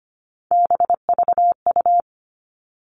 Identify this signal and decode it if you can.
Morse code, heard as a keyed tone, about 25 wpm, 700 Hz. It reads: BVU
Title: All Cut Numbers - 3-Digit 25wpm